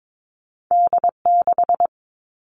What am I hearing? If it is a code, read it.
Morse code, heard as a keyed tone, about 22 wpm, 700 Hz. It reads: D6
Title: All Cut Numbers - 2-Digit 22wpm